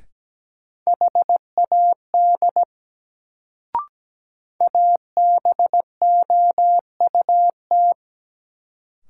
Morse code, heard as a keyed tone, about 17 wpm, 700 Hz. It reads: HAD ABOUT
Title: Sentences from Top 100 Words - New Word Drill - 17wpm